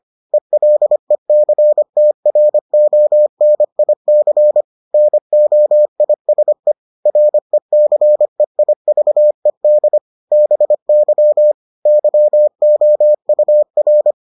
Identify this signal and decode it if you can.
Morse code, heard as a keyed tone, about 25 wpm, 600 Hz. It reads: ELECTRONIC NOISE RECEIVED BY YOUR